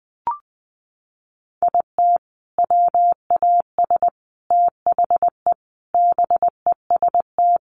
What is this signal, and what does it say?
Morse code, heard as a keyed tone, about 20 wpm, 700 Hz. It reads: IT WAS THE BEST
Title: Sentences from Top 400 Words 20wpm